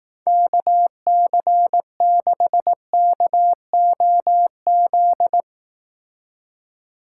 Morse code, heard as a keyed tone, about 18 wpm, 700 Hz. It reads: KC6KOZ
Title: Speed Racing - USA Call Signs 15wpm